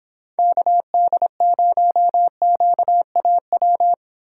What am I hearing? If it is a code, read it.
Morse code, heard as a keyed tone, about 26 wpm, 700 Hz. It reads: KD0QAW